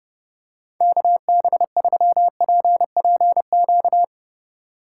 Morse code, heard as a keyed tone, about 30 wpm, 700 Hz. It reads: KB3PPQ
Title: USA Call Signs - 3x Character Spacing - 30wpm